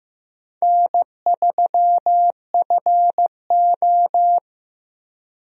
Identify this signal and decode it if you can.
Morse code, heard as a keyed tone, about 15 wpm, 700 Hz. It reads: N3FO
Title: USA Call Signs - 4 Characters - Repeated 2x - 15wpm